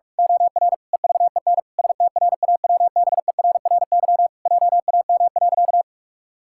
Morse code, heard as a keyed tone, about 44 wpm, 700 Hz. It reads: OR EVEN STRAWBERRY JAM.
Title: Pangram Challenge v2 - 45wpm